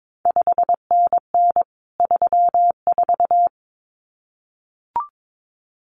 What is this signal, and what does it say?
Morse code, heard as a keyed tone, about 22 wpm, 700 Hz. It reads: 5NN 34
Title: CQ WW DX Contest Exchange Encore - 22wpm